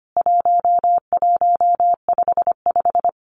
Morse code, heard as a keyed tone, about 25 wpm, 700 Hz. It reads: 1155
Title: CWT Exchange 25wpm